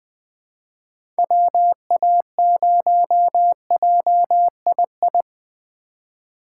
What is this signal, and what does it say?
Morse code, heard as a keyed tone, about 21 wpm, 700 Hz. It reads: WA0JII